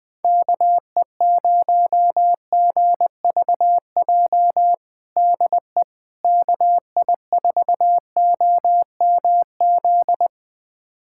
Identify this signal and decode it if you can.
Morse code, heard as a keyed tone, about 20 wpm, 700 Hz. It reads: KE0GVJ DE KI4OMZ
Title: USA Callsign DE Callsign 20wpm